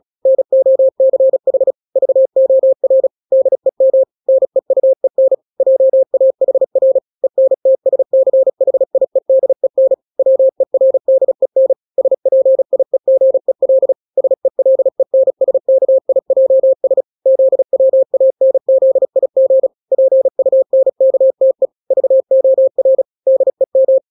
Morse code, heard as a keyed tone, about 36 wpm, 530 Hz. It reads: NOCH VOR DEM NEUEN JAHR ENTSCHIEDEN WERDEN SPIEGEL SELENSKIJS ZWANZIG PUNKTE VOR DEM
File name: daily / 35wpm